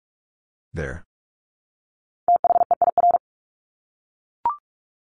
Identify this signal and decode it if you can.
Morse code, heard as a keyed tone, about 44 wpm, 700 Hz. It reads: THEIR
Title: Top 200 Words Repeated 45wpm